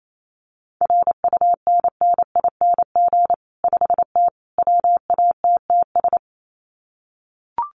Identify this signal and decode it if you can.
Morse code, heard as a keyed tone, about 28 wpm, 700 Hz. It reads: RUNNING 5T WATTS
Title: Common QSO Phrases 28wpm